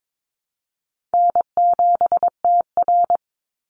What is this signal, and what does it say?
Morse code, heard as a keyed tone, about 22 wpm, 700 Hz. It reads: N7TR
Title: Call Signs Repeated 2x - 22wpm